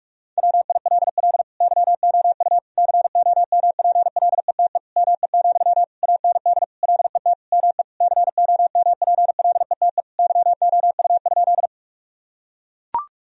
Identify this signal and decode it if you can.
Morse code, heard as a keyed tone, about 42 wpm, 700 Hz. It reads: WILL YOU COMPLETE ME, AND LET ME COMPLETE YOU?
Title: Sentences from Top 1000 Words - Review Part 2 - 45wpm